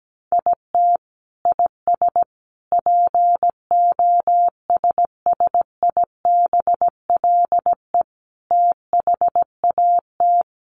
Morse code, heard as a keyed tone, about 17 wpm, 700 Hz. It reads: IT IS POSSIBLE THAT
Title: Sets of 5 Words Encore2 17wpm